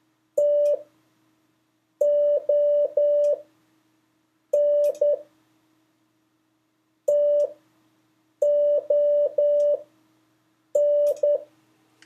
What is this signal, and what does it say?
Morse code, heard as a keyed tone, about 10 wpm, spaced as for 6 wpm, 580 Hz. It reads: TON TON
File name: 4of5-words-3